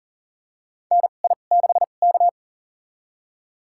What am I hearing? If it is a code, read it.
Morse code, heard as a keyed tone, about 40 wpm, spaced as for 29 wpm, 700 Hz. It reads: NI6K
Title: USA Call Signs - 4 Characters - 2x Character Spacing - 40wpm